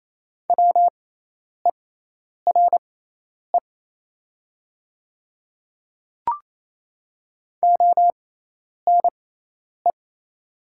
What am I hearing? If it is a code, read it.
Morse code, heard as a keyed tone, about 28 wpm, spaced as for 10 wpm, 700 Hz. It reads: WERE ONE
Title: 6x Character Spacing - Top 100 Words - 28wpm